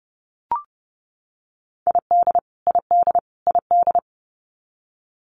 Morse code, heard as a keyed tone, about 30 wpm, 700 Hz. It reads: ID ID ID
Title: US States Repeated 3x 30wpm